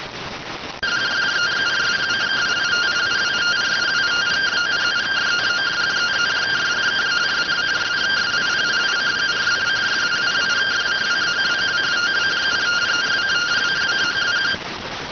Two-tone teleprinter signal, about 170 Hz shift.